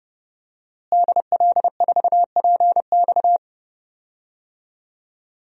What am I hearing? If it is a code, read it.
Morse code, heard as a keyed tone, about 32 wpm, 700 Hz. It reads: DL4PX
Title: DXCC-100 Call Signs - 5 Characters - Repeated 3x - 30wpm